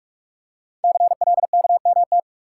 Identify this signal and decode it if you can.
Morse code, heard as a keyed tone, about 44 wpm, 700 Hz. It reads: CLKMT
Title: Code Groups - 5 Letters - 45wpm